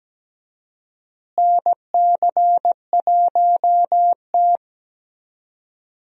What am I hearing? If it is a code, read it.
Morse code, heard as a keyed tone, about 17 wpm, 700 Hz. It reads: NC1T
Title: USA Call Signs - 4 Characters - Repeated 3x - 17wpm